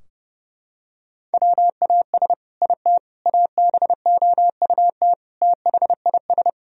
Morse code, heard as a keyed tone, about 32 wpm, 700 Hz. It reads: WAS IT ABOUT THIS
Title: Sentences from Top 1000 Words - Review Part 3 - 30wpm